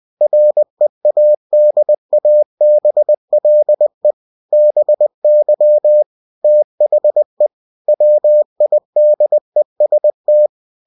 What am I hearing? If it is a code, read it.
Morse code, heard as a keyed tone, about 20 wpm, 600 Hz. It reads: READABLE BY THE WIDEST
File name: cw-books / winnie-the-pooh / winnie-20wpm-page-0120